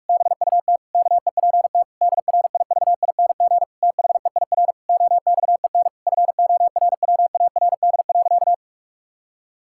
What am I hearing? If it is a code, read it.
Morse code, heard as a keyed tone, about 41 wpm, 700 Hz. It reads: BUT KEPT DRIVING THEIR OXEN FORWARD.
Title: Pangram Challenge v2 - 45wpm